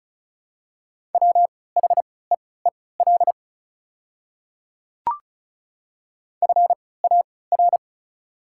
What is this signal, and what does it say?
Morse code, heard as a keyed tone, about 36 wpm, spaced as for 20 wpm, 700 Hz. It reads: WHEEL FAR
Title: 3x Character Spacing - Top 500 Words - 35wpm